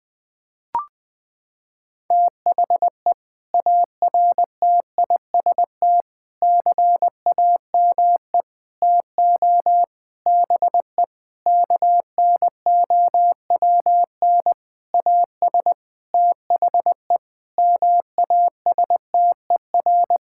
Morse code, heard as a keyed tone, about 20 wpm, 700 Hz. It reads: THE ARTIST CAME TO BE KNOWN AS THE MASTER
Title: Sentences with 6-Word Phrases - 20wpm